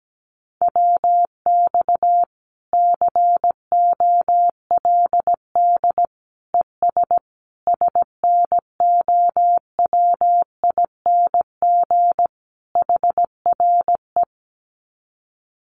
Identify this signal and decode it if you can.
Morse code, heard as a keyed tone, about 17 wpm, 700 Hz. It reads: WX COLD ES SNOWING HRE
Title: Common QSO Phrases 17wpm